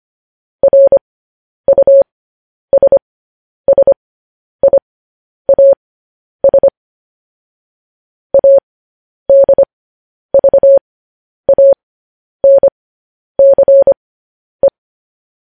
Morse code, heard as a keyed tone, about 25 wpm, spaced as for 10 wpm, 550 Hz. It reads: RUSSIAS ADVANCE
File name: podcast / News headlines-1x-25x10